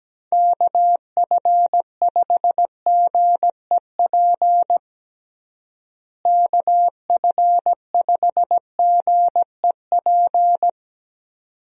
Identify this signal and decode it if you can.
Morse code, heard as a keyed tone, about 17 wpm, 700 Hz. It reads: KF5GEP KF5GEP
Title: USA Call Signs - 6 Characters - Repeated 2x - 17wpm